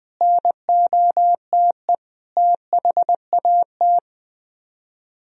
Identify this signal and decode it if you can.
Morse code, heard as a keyed tone, about 19 wpm, 700 Hz. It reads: NOTE THAT
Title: Speed-Racing - Sets of 5 Words 20wpm